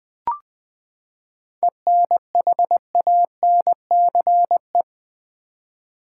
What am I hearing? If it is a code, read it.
Morse code, heard as a keyed tone, about 20 wpm, 700 Hz. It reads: ENHANCE
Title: Seven Letter Words - 20wpm